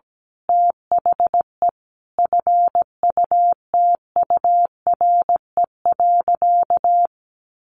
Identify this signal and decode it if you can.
Morse code, heard as a keyed tone, about 17 wpm, 700 Hz. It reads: THE FUTURE.
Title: Sentences from Top 1400 Words 17wpm